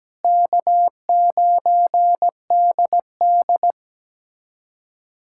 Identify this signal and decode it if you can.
Morse code, heard as a keyed tone, about 17 wpm, 700 Hz. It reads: K9DD
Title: USA Callsign DE Callsign - 4 Characters - 17wpm